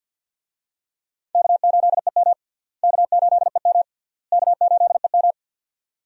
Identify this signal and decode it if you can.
Morse code, heard as a keyed tone, about 51 wpm, 700 Hz. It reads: K8EM K8EM K8EM
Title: USA Call Signs - 4 Characters - Repeated 3x - 50wpm